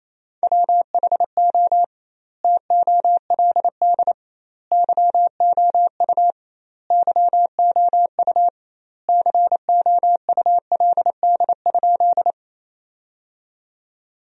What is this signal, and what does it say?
Morse code, heard as a keyed tone, about 28 wpm, 700 Hz. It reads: WHO TOLD YOU YOU COULD?
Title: 2x Word-Spacing - Sentences from Top 400 Words - 28wpm